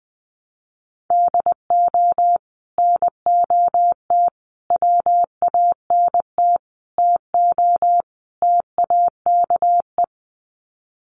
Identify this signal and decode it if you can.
Morse code, heard as a keyed tone, about 20 wpm, 700 Hz. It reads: DO NOT WANT TO TAKE
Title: Sets of 5 Words Encore2 20wpm